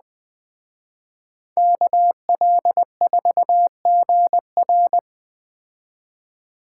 Morse code, heard as a keyed tone, about 20 wpm, 700 Hz. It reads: KL4GR
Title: Speed Racing - USA 2x2 Call Signs 17wpm